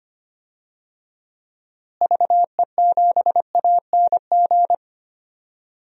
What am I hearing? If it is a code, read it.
Morse code, heard as a keyed tone, about 25 wpm, 700 Hz. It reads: VE7ANG
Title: DXCC-100 Call Signs - 6 Characters - Repeated 3x - 25wpm